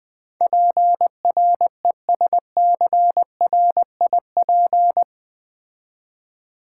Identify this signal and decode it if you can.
Morse code, heard as a keyed tone, about 20 wpm, 700 Hz. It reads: PRESCRIP
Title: Build A Word - 4 Letters At A Time - 12-Letter Words - 20wpm